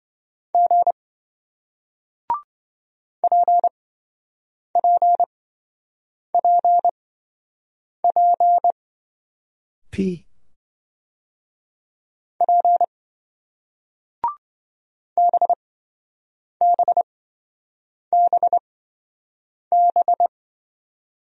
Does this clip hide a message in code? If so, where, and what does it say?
Yes: Morse code from 0.5 to 20.3 s, heard as a keyed tone, about 27 wpm, spaced as for 7 wpm, 700 Hz. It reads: G PPPP P BBBB